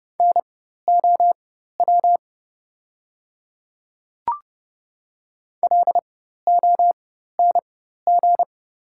Morse code, heard as a keyed tone, about 30 wpm, spaced as for 14 wpm, 700 Hz. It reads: NOW LONG
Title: 4x Character Spacing - Top 100 Words - 30wpm